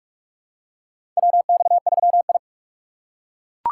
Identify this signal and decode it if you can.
Morse code, heard as a keyed tone, about 42 wpm, 700 Hz. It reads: WX3I
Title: USA Call Signs - 4x Character Spacing - 45wpm